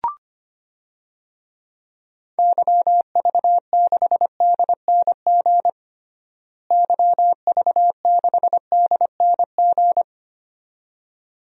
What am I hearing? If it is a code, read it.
Morse code, heard as a keyed tone, about 25 wpm, 700 Hz. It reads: YV6DNG YV6DNG
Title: DXCC-100 Call Signs - 6 Characters - Repeated 2x - 25wpm